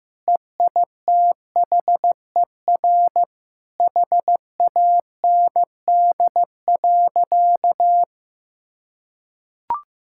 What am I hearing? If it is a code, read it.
Morse code, heard as a keyed tone, about 15 wpm, 700 Hz. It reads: EITHER HAND.
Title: Sentences from Top 1600 Words 15wpm